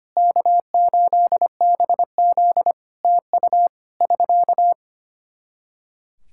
Morse code, heard as a keyed tone, about 25 wpm, 700 Hz. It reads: K8BZ TU <SK>
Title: Practice QSO - Exchange 4 - Ending - 25wpm